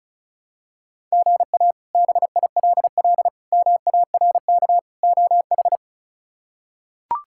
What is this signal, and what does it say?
Morse code, heard as a keyed tone, about 34 wpm, 700 Hz. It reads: GA BILL MARK OH
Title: K1USN SST QSO Practice - 35wpm